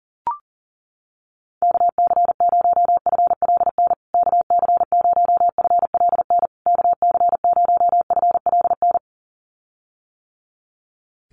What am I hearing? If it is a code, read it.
Morse code, heard as a keyed tone, about 40 wpm, 700 Hz. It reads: KC0FLN KC0FLN KC0FLN
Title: USA Call Signs Repeated 3x 40wpm